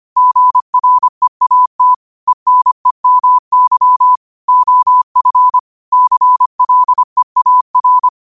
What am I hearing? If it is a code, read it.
Morse code, heard as a keyed tone, about 25 wpm, 1000 Hz. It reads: GREAT ENEMY OF CLEAR
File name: qotdfiles / 25wpm